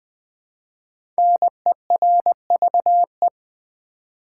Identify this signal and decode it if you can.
Morse code, heard as a keyed tone, about 20 wpm, 700 Hz. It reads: NERVE
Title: Five Letter Words - 20wpm